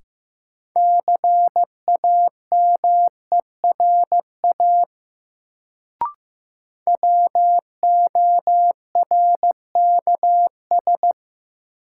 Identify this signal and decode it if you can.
Morse code, heard as a keyed tone, about 15 wpm, 700 Hz. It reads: CAMERA WORKS
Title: Sentences from Top 1000 Words - Review Part 5 - New Word Drill - 15wpm